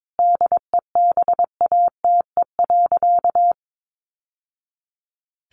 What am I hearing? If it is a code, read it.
Morse code, heard as a keyed tone, about 22 wpm, 700 Hz. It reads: DEBATE.
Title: Sentences from Top 1100 Words 22wpm